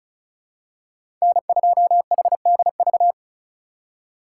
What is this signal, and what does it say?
Morse code, heard as a keyed tone, about 36 wpm, 700 Hz. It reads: N2HDV
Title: USA Call Signs - 5 Characters - 2x Character Spacing - 35wpm